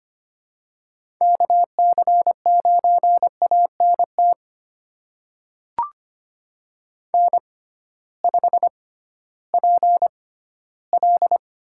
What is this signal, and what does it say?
Morse code, heard as a keyed tone, about 25 wpm, spaced as for 15 wpm, 700 Hz. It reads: KC9ANT N5PL
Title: USA Call Signs - 6x Character Spacing - 25wpm